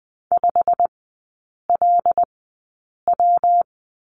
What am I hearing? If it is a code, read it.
Morse code, heard as a keyed tone, about 20 wpm, spaced as for 8 wpm, 700 Hz. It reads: 5LW
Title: 133 Lesson 22e - Question Mark - Three Characters 20wpm